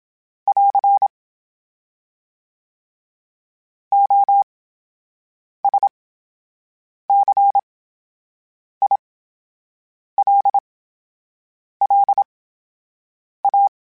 Morse code, heard as a keyed tone, about 27 wpm, spaced as for 7 wpm, 800 Hz. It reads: <AR> OSCILLA
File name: Special INGs / Special INGs-mp3s / VerySlowToFaster Ing @25